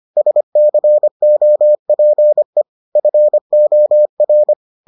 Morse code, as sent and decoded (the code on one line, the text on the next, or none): ... -.-. --- .--. / ..-. --- .-.
SCOPE FOR